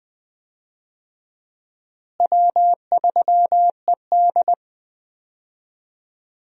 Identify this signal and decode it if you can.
Morse code, heard as a keyed tone, about 20 wpm, 700 Hz. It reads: W3ED
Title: USA Call Signs - 4 Characters - Repeated 2x - 20wpm